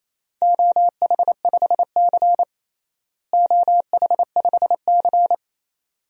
Morse code, heard as a keyed tone, about 28 wpm, 700 Hz. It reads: OH5C OH5C
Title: DXCC-100 Call Signs - 4 Characters - Repeated 2x - 28wpm